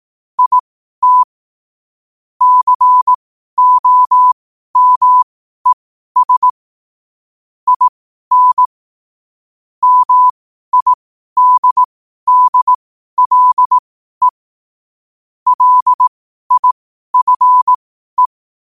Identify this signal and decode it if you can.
Morse code, heard as a keyed tone, about 18 wpm, spaced as for 12 wpm, 1000 Hz. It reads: IT COMES IN MIDDLE LIFE
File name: qotdfiles / 12wpm